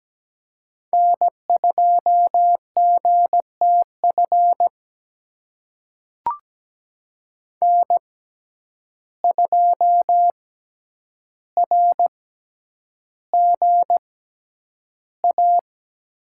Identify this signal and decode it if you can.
Morse code, heard as a keyed tone, about 17 wpm, spaced as for 9 wpm, 700 Hz. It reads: N2GTF N2RGA
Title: USA Call Signs - 5 Characters - 6x Character Spacing - 17wpm